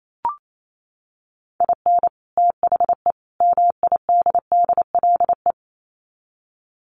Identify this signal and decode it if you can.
Morse code, heard as a keyed tone, about 28 wpm, 700 Hz. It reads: IN THE MIDDLE
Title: Sets of 3 Words 28wpm